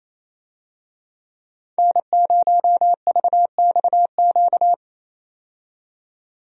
Morse code, heard as a keyed tone, about 27 wpm, 700 Hz. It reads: N0VXQ
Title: USA Call Signs - 5 Characters - 28wpm